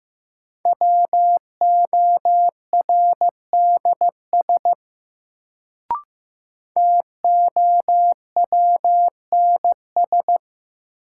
Morse code, heard as a keyed tone, about 15 wpm, 700 Hz. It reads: WORDS TOWNS
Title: Sentences from Top 1000 Words - Review Part 6 - New Word Drill - 15wpm